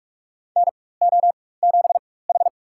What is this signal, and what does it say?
Morse code, heard as a keyed tone, about 45 wpm, spaced as for 21 wpm, 700 Hz. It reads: NO7H
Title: USA Call Signs - 4 Characters - 4x Character Spacing - 45wpm